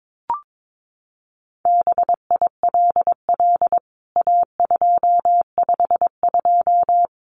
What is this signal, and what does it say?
Morse code, heard as a keyed tone, about 22 wpm, 700 Hz. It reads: BILL A252
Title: CWT Exchange 22wpm